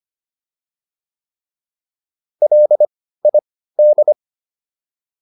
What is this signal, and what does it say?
Morse code, heard as a keyed tone, about 25 wpm, spaced as for 15 wpm, 600 Hz. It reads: LID